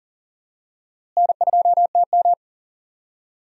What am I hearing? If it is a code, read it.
Morse code, heard as a keyed tone, about 40 wpm, 700 Hz. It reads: N2TM
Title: USA Call Signs - 4 Characters - Repeated 3x - 40wpm